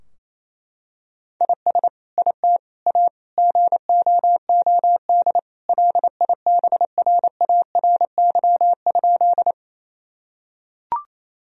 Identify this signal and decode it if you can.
Morse code, heard as a keyed tone, about 28 wpm, 700 Hz. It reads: IS IT A GOOD LIBRARY?
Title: Sentences from Top 1000 Words - Review Part 4 - 28wpm